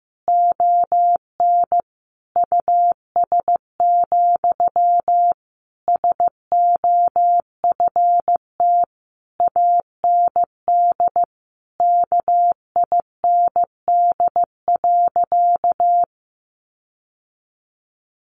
Morse code, heard as a keyed tone, about 15 wpm, 700 Hz. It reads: ON US, SOFT AND KIND.
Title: Sentences from Top 1500 Words 15wpm